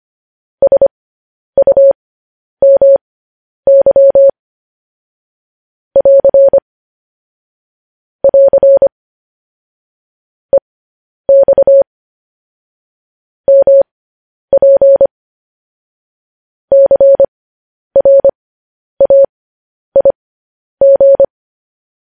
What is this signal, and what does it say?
Morse code, heard as a keyed tone, about 25 wpm, spaced as for 10 wpm, 550 Hz. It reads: SUMY <AR> <AR> EX MP CRAIG
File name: podcast / News headlines-1x-25x10